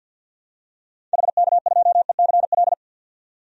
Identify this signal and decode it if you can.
Morse code, heard as a keyed tone, about 51 wpm, 700 Hz. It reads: HK2ECL
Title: DXCC-100 Call Signs - 6 Characters - 50wpm